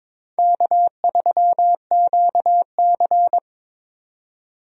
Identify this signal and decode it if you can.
Morse code, heard as a keyed tone, about 21 wpm, 700 Hz. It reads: K3QC
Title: USA Call Signs - 4 Characters - 22wpm